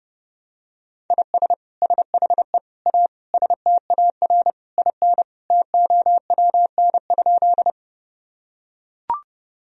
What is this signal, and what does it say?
Morse code, heard as a keyed tone, about 30 wpm, 700 Hz. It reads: IS SHE A STAR IN TOWN?
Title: Sentences from Top 1000 Words - Review Part 2 - 30wpm